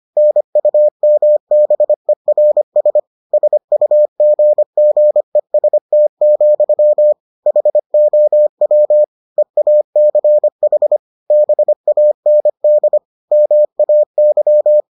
Morse code, heard as a keyed tone, about 25 wpm, 600 Hz. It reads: NUMBERS SUGGEST, HOW EACH BAND MAY